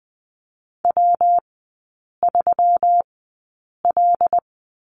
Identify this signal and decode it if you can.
Morse code, heard as a keyed tone, about 20 wpm, spaced as for 8 wpm, 700 Hz. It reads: W3L
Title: 198 Lesson 32e - Letter k - Three Characters 20wpm